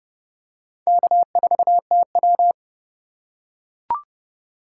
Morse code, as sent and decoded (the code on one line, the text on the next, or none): -.- ....- - .--
K4TW